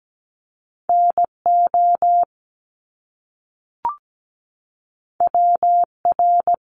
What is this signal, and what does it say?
Morse code, heard as a keyed tone, about 17 wpm, 700 Hz. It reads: NO WR
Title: Top Two Letter Combinations in Words 17wpm